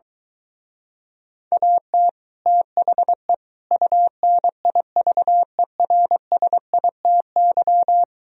Morse code, heard as a keyed tone, about 23 wpm, 700 Hz. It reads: AT THE UNIVERSITY